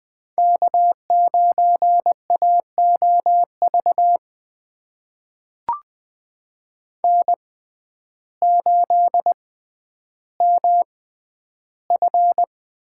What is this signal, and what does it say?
Morse code, heard as a keyed tone, about 20 wpm, spaced as for 11 wpm, 700 Hz. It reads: K9AOV N8MF
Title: USA Call Signs - 5 Characters - 6x Character Spacing - 20wpm